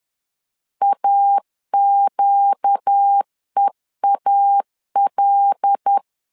Morse code, heard as a keyed tone, about 10 wpm, 800 Hz. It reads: AQEAL
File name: LEZIONI / 5 LBVC / 5 L 10wpm 2259